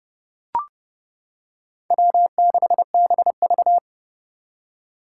Morse code, heard as a keyed tone, about 30 wpm, 700 Hz. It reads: W6BV
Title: Four Challenging Letter-Number Combinations 30wpm